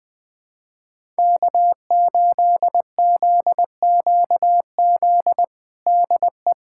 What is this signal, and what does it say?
Morse code, heard as a keyed tone, about 20 wpm, 700 Hz. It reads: K8ZQZ DE
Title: USA Callsign DE Callsign - 5 Characters - 20wpm